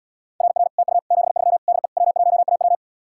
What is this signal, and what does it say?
Morse code, heard as a keyed tone, about 77 wpm, 700 Hz. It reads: RR TU 73 DE W1AW